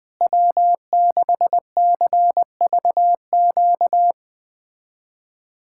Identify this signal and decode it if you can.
Morse code, heard as a keyed tone, about 20 wpm, 700 Hz. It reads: W6CVQ
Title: USA Callsign DE Callsign 20wpm